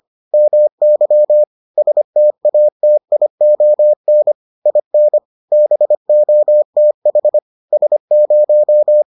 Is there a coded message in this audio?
Yes: Morse code from 0.3 to 9.0 s, heard as a keyed tone, about 25 wpm, 600 Hz. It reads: MY STATION IN BOTH S0